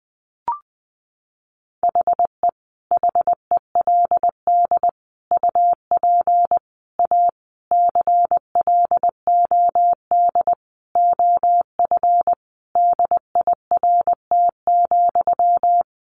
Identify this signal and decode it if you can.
Morse code, heard as a keyed tone, about 19 wpm, 700 Hz. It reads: HE HELD UP A CLOD OF DIRT,
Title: A Sound Of Thunder 20wpm